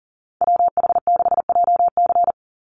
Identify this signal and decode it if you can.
Morse code, heard as a keyed tone, about 40 wpm, 700 Hz. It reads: WH6JC
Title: USA Call Signs 40wpm